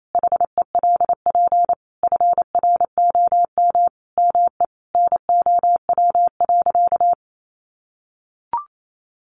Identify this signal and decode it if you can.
Morse code, heard as a keyed tone, about 28 wpm, 700 Hz. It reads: HELP FROM ME NOW.
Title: Sentences from Top 1300 Words 28wpm